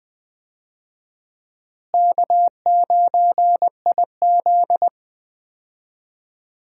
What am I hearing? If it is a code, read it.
Morse code, heard as a keyed tone, about 20 wpm, 700 Hz. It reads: K9IZ
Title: USA Call Signs - 4 Characters - Repeated 2x - 20wpm